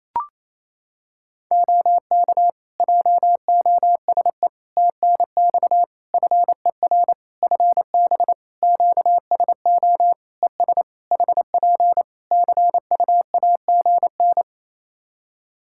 Morse code, heard as a keyed tone, about 28 wpm, 700 Hz. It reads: OK JOSE TNX FER FB QSO ES HP CUAGN
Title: K1USN SST QSO Practice 28wpm